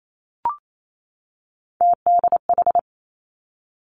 Morse code, heard as a keyed tone, about 28 wpm, 700 Hz. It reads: TDH
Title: Three Challenging Letter-Number Combinations 28wpm